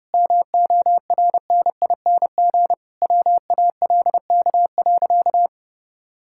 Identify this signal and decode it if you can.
Morse code, heard as a keyed tone, about 30 wpm, 700 Hz. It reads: MORNING WALK.
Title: Sentences from Top 1000 Words - Review Part 5 - 30wpm